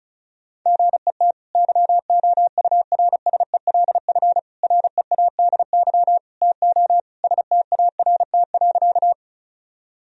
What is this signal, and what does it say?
Morse code, heard as a keyed tone, about 34 wpm, 700 Hz. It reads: GET YOURSELF READY TO START.
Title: Sentences from Top 1000 Words - Review Part 1 - 35wpm